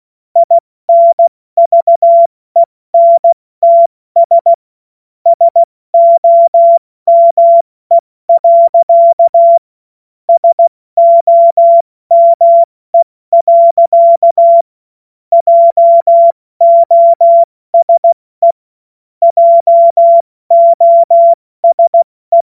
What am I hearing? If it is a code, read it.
Morse code, heard as a keyed tone, about 16 wpm, 680 Hz. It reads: INVENTS SOME. SOME. JOSE JOSE